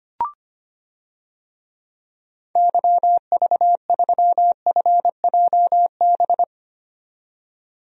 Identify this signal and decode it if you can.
Morse code, heard as a keyed tone, about 25 wpm, 700 Hz. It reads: YV3FJB
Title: DXCC-100 Call Signs - 6 Characters - 25wpm